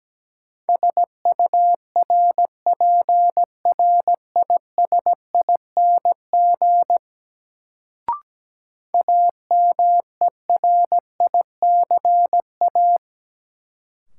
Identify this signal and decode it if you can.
Morse code, heard as a keyed tone, about 17 wpm, 700 Hz. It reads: SURPRISING AMERICA
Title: Sets of 5 Words - New Word Drill - 17wpm